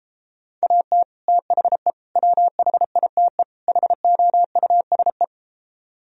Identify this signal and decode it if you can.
Morse code, heard as a keyed tone, about 33 wpm, 700 Hz. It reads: AT THE WHITE HOUSE